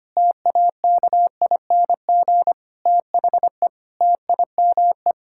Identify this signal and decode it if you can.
Morse code, heard as a keyed tone, about 25 wpm, 700 Hz. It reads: TAKING THE TIME
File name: Sets of 9 Words - 25wpm